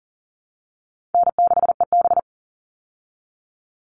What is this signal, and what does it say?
Morse code, heard as a keyed tone, about 40 wpm, 700 Hz. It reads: N6EB
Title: All Cut Numbers - 4-Digit 40wpm